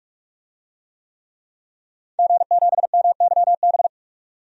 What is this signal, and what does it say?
Morse code, heard as a keyed tone, about 44 wpm, 700 Hz. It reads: G7MYB
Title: DXCC-100 Call Signs - 5 Characters - 45wpm